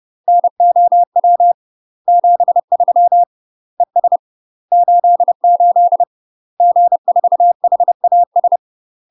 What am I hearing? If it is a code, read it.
Morse code, heard as a keyed tone, about 30 wpm, 700 Hz. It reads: NOW 73 ES 88 G4HAS
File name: UK MASTER QSOs TRAINING / 30WPM / UK MASTER QSOs 0020